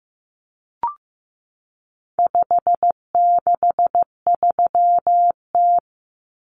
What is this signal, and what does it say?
Morse code, heard as a keyed tone, about 15 wpm, 700 Hz. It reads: 563T
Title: Common Cut Numbers - 4-Digit 15wpm